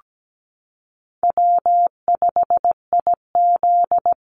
Morse code, heard as a keyed tone, about 17 wpm, 700 Hz. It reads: W5IZ
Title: Call Signs Repeated 3x - 17wpm